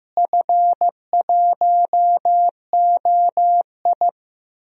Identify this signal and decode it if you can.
Morse code, heard as a keyed tone, about 15 wpm, 700 Hz. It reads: F1OI
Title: DXCC-100 Call Signs - 4 Characters - Repeated 3x - 15wpm